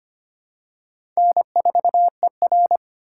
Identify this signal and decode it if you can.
Morse code, heard as a keyed tone, about 25 wpm, 690 Hz. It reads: N4ER
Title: USA Call Signs - 4 Characters - 4x Character Spacing - 25wpm